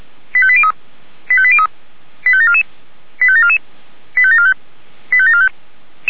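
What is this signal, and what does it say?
Five-tone selective call (ZVEI), heard as quick runs of five tones, tones of about 70 ms. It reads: 85803 85803 85744 85744 85745 85745
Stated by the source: ZVEI five-tone selective call received via SDR.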